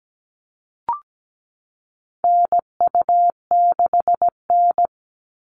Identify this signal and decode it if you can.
Morse code, heard as a keyed tone, about 17 wpm, 700 Hz. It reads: NU6N
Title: Call Signs Repeated 3x - 17wpm